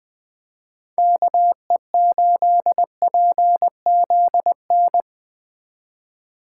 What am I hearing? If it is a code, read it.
Morse code, heard as a keyed tone, about 20 wpm, 700 Hz. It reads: KE8PZN
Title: 2023 POTA Activator Call Signs - 20wpm